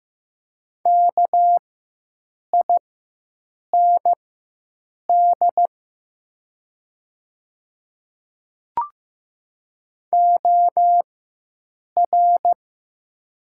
Morse code, heard as a keyed tone, about 15 wpm, spaced as for 7 wpm, 700 Hz. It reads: KIND OR
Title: 4x Character Spacing - Top 200 Words - 15wpm